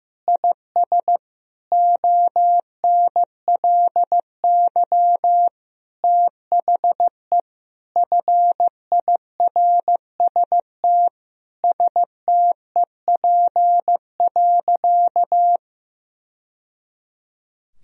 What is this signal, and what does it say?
Morse code, heard as a keyed tone, about 15 wpm, 700 Hz. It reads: IS ONLY THE FIRST STEP.
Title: Sentences from Top 1000 Words - Review Part 2 - 15wpm